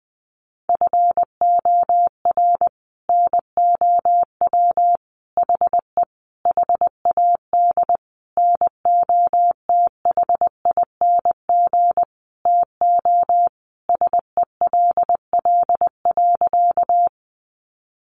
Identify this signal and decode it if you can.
Morse code, heard as a keyed tone, about 20 wpm, 700 Hz. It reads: FOR NOW HE HAD NOTHING TO SELL.
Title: Sentences from Top 1100 Words 20wpm